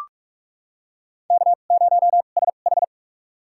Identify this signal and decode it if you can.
Morse code, heard as a keyed tone, about 46 wpm, spaced as for 32 wpm, 700 Hz. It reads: K0SH
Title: USA Call Signs - 4 Characters - 2x Character Spacing - 45wpm